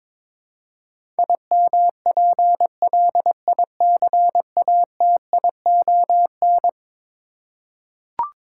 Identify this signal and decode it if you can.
Morse code, heard as a keyed tone, about 21 wpm, 700 Hz. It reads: IMPLICATION
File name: practice / 22wpm / Eleven Letter Words - 22wpm